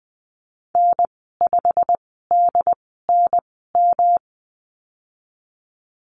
Morse code, heard as a keyed tone, about 20 wpm, spaced as for 15 wpm, 700 Hz. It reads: N5DNM
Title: 154 Lesson 25g - Letter p - Call Signs 20wpm